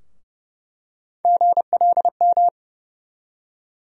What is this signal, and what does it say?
Morse code, heard as a keyed tone, about 31 wpm, 700 Hz. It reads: GLM
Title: USA Call Sign Postfixes - 3 Letters 30wpm